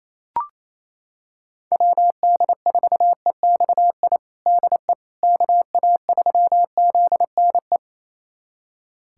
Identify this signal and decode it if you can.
Morse code, heard as a keyed tone, about 28 wpm, 700 Hz. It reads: WD4EXI DE KA3ZNE
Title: USA Callsign DE Callsign - 6 Characters - 28wpm